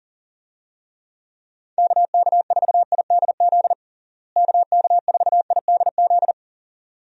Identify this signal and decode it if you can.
Morse code, heard as a keyed tone, about 40 wpm, 700 Hz. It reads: KK4IDZ KK4IDZ
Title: USA Call Signs - 6 Characters - Repeated 2x - 40wpm